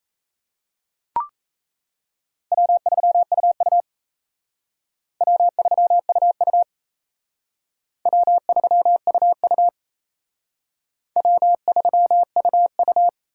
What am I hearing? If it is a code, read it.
Morse code, heard as a keyed tone, about 35 wpm, 700 Hz. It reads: W3UU W3UU W3UU W3UU
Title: Speed Racing - USA 1x2 Call Signs 28wpm